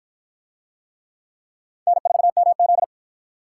Morse code, heard as a keyed tone, about 50 wpm, 700 Hz. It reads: N4MZ